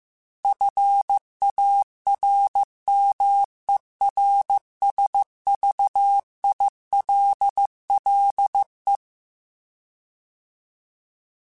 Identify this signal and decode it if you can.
Morse code, heard as a keyed tone, about 15 wpm, 800 Hz. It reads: FARMERSVILLE